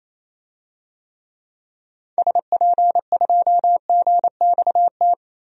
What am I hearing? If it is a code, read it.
Morse code, heard as a keyed tone, about 28 wpm, 700 Hz. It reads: SP2GXT
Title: DXCC-100 Call Signs - 6 Characters - Repeated 2x - 28wpm